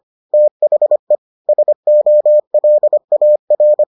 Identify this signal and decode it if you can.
Morse code, heard as a keyed tone, about 25 wpm, 600 Hz. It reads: THE SOLAR